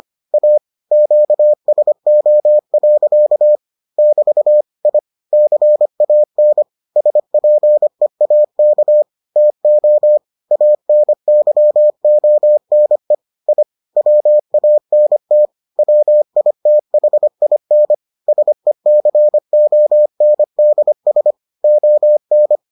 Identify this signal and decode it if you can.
Morse code, heard as a keyed tone, about 25 wpm, 600 Hz. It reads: A QSO. <BT> I CAN SPEAK TO ANYONE I WANT WITHIN SECONDS ON